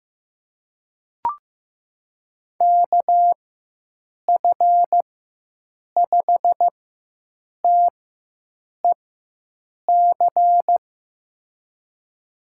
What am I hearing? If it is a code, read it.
Morse code, heard as a keyed tone, about 15 wpm, spaced as for 7 wpm, 700 Hz. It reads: KF5TEC
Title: USA Call Signs - 6 Characters - 4x Character Spacing - 15wpm